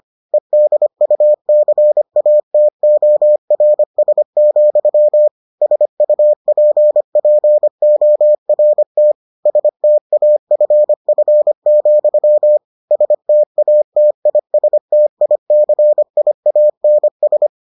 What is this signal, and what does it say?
Morse code, heard as a keyed tone, about 25 wpm, 600 Hz. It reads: EDUCATORS, SUPPORT STAFF, STATISTICIANS